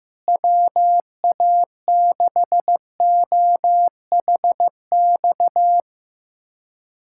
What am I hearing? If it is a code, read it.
Morse code, heard as a keyed tone, about 15 wpm, 700 Hz. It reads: WA6OHX
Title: USA Callsign DE Callsign - 6 Characters - 15wpm